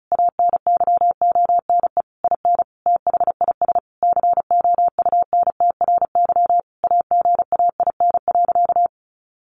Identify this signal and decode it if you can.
Morse code, heard as a keyed tone, about 34 wpm, 700 Hz. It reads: ANYONE IN THIS COUNTRY AGAIN.
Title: Sentences from Top 1200 Words 35wpm